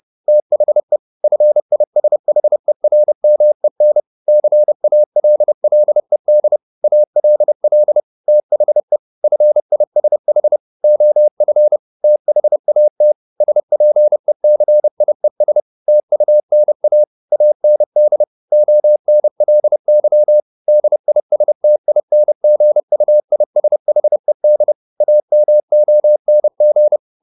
Morse code, heard as a keyed tone, about 30 wpm, 600 Hz. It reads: THE FISHERMEN CALLED ALL THE FISH OF THAT SPECIES TUNA AND ONLY DISTINGUISHED AMONG
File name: the old man and the sea / the old man and the sea 1 30wpm